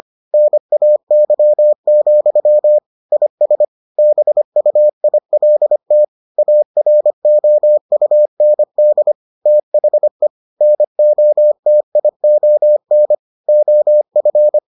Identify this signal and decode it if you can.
Morse code, heard as a keyed tone, about 25 wpm, 600 Hz. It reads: NAY, IS BUILT AROUND THE NOTION OF